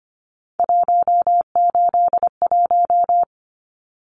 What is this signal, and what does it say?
Morse code, heard as a keyed tone, about 25 wpm, 700 Hz. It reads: 181
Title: Numbers - 3-Digit 25wpm